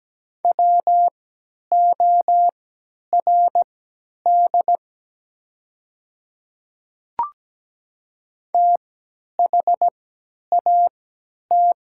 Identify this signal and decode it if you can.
Morse code, heard as a keyed tone, about 17 wpm, spaced as for 10 wpm, 700 Hz. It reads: WORD THAT
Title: 3x Character Spacing - Top 100 Words - 17wpm